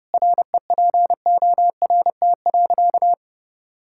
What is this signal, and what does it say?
Morse code, heard as a keyed tone, about 30 wpm, 700 Hz. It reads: REPORT.
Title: Sentences from Top 1000 Words - Review Part 6 - 30wpm